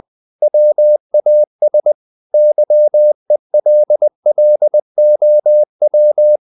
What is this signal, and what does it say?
Morse code, heard as a keyed tone, about 20 wpm, 600 Hz. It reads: WAS YELLOW
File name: the old man and the sea / the old man and the sea part 1 20wpm